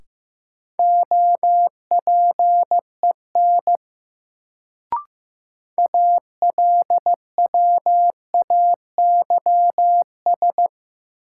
Morse code, heard as a keyed tone, about 15 wpm, 700 Hz. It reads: OPEN ALWAYS
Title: Sentences from Top 300 Words - New Word Drill - 15wpm